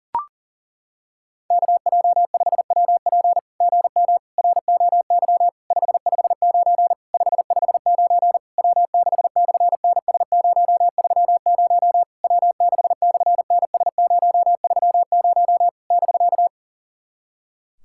Morse code, heard as a keyed tone, about 40 wpm, 700 Hz. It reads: KJ5WP GM ROY 559 559 W6/NS030 W6/NS030 <BK>
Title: SOTA QSO Practice 40wpm